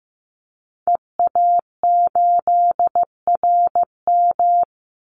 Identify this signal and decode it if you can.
Morse code, heard as a keyed tone, about 15 wpm, 700 Hz. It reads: EA8RM
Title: Call Signs Repeated 2x - 15wpm